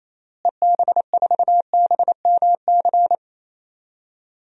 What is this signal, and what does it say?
Morse code, heard as a keyed tone, about 28 wpm, 700 Hz. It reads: EB4BMC
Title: DXCC-100 Call Signs - 6 Characters - Repeated 3x - 28wpm